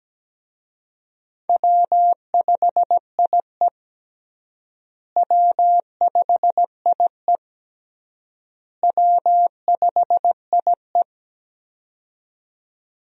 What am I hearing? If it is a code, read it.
Morse code, heard as a keyed tone, about 17 wpm, 700 Hz. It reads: W5IE W5IE W5IE
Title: USA Call Signs - 4 Characters - Repeated 3x - 17wpm